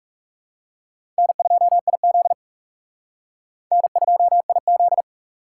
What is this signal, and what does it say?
Morse code, heard as a keyed tone, about 36 wpm, 700 Hz. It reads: N2IZ N2IZ
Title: Speed Racing - USA 1x2 Call Signs 30wpm